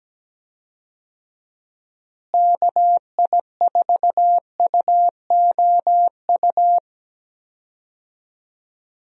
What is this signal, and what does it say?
Morse code, heard as a keyed tone, about 17 wpm, 700 Hz. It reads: KI4UOU
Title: USA Call Signs - 6 Characters - Repeated 3x - 17wpm